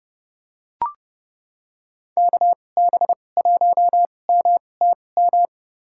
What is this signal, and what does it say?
Morse code, heard as a keyed tone, about 30 wpm, spaced as for 22 wpm, 700 Hz. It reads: KB1MTM
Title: USA Call Signs - 6 Characters - 2x Character Spacing - 30wpm